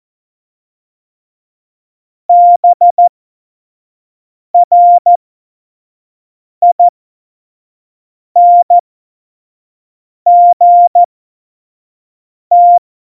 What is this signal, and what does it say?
Morse code, heard as a keyed tone, about 14 wpm, spaced as for 5 wpm, 700 Hz. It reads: BRINGT
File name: Morsekurs / Witze / Witz 25 Juli 17, Tempo 25 F70